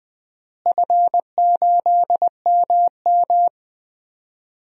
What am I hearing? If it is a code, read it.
Morse code, heard as a keyed tone, about 20 wpm, 700 Hz. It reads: F8MM
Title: DXCC-100 Call Signs - 4 Characters - Repeated 3x - 20wpm